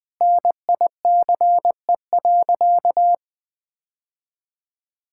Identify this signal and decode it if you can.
Morse code, heard as a keyed tone, about 20 wpm, 700 Hz. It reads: NICE.
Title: Sentences with 2-Word Phrases - 20wpm